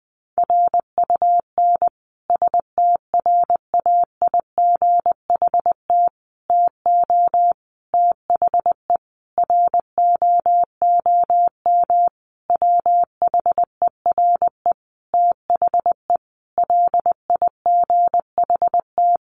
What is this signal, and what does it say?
Morse code, heard as a keyed tone, about 20 wpm, 700 Hz. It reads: RUN STRAIGHT TO THE ROOM WHERE THE LIGHT
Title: Sentences from Top 1000 Words 20wpm